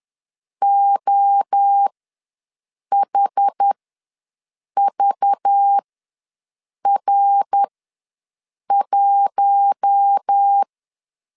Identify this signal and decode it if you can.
Morse code, heard as a keyed tone, about 10 wpm, spaced as for 6 wpm, 800 Hz. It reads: OHVR1
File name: LEZIONI / 5 LBVC / 5 LBV ws 10wpm 3645